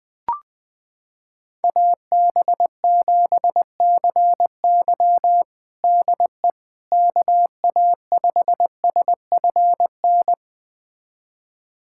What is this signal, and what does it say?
Morse code, heard as a keyed tone, about 20 wpm, 700 Hz. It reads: AB7CY DE KA5SFN
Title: USA Callsign DE Callsign 20wpm